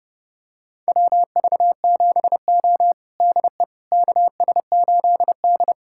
Morse code, heard as a keyed tone, about 32 wpm, 700 Hz. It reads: WV7O DE KS8D
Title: USA Callsign DE Callsign - 4 Characters - 30wpm